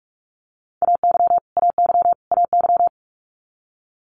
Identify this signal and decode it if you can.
Morse code, heard as a keyed tone, about 46 wpm, 700 Hz. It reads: AY AY AY
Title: Multi-letter Phonemes Repeated 3x 45wpm